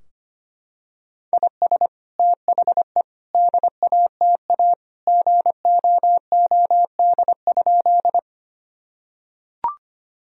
Morse code, heard as a keyed tone, about 25 wpm, 700 Hz. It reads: IS THE DATA GOOD?
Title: Sentences from Top 1000 Words - Review Part 3 - 25wpm